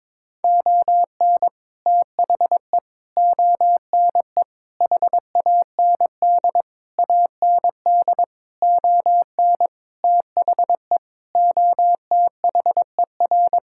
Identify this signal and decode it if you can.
Morse code, heard as a keyed tone, about 22 wpm, 700 Hz. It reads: ON THE ONE HAND AND ON THE OTHER
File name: practice / 22wpm / Sets of 8 Words - 22wpm